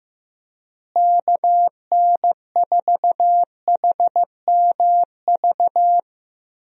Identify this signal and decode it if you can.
Morse code, heard as a keyed tone, about 15 wpm, 700 Hz. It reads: KN4HMV
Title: USA Call Signs - 6 Characters - Repeated 3x - 15wpm